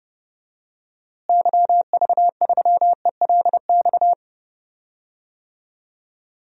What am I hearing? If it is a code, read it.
Morse code, heard as a keyed tone, about 31 wpm, 700 Hz. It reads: YV3ELX
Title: DXCC-100 Call Signs - 6 Characters - Repeated 2x - 30wpm